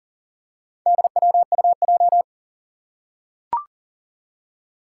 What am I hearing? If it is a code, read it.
Morse code, heard as a keyed tone, about 40 wpm, 700 Hz. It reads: DWUJ
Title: Four Challenging Letter-Number Combinations 40wpm